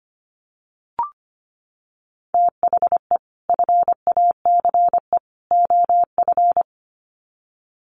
Morse code, heard as a keyed tone, about 25 wpm, 700 Hz. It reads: THE FACE OF
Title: Sets of 3 Words 25wpm